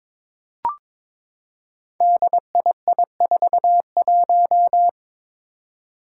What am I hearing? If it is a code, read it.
Morse code, heard as a keyed tone, about 22 wpm, 700 Hz. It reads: DII41
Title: Five Challenging Letter-Number Combinations 22wpm